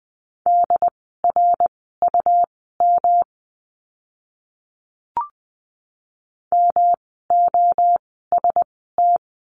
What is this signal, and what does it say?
Morse code, heard as a keyed tone, about 20 wpm, spaced as for 15 wpm, 700 Hz. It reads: DRUM MOST
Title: 108 Lesson 18f - Letter m - Words 20wpm